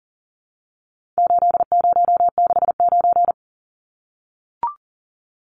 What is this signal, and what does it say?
Morse code, heard as a keyed tone, about 39 wpm, 700 Hz. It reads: 8069
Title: Numbers - 4-Digit 40wpm